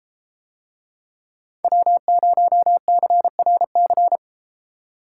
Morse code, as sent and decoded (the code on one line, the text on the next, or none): .-- ----- -.-. .-. -.-.
W0CRC